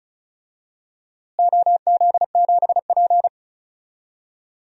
Morse code, heard as a keyed tone, about 36 wpm, 700 Hz. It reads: OZ7P
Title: DXCC-100 Call Signs - 4 Characters - Repeated 2x - 35wpm